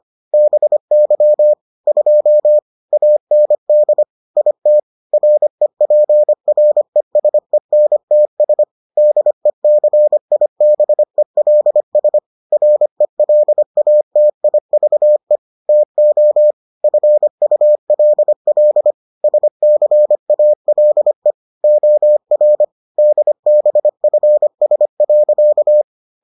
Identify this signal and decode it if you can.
Morse code, heard as a keyed tone, about 25 wpm, 600 Hz. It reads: BY 2 AND IT REPRESENTS DECIBELS RELATIVE TO FULL SCALE OR DBFS.